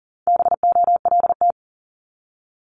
Morse code, heard as a keyed tone, about 38 wpm, 700 Hz. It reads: BOLT
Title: Multi-letter Phonemes Repeated 3x 40wpm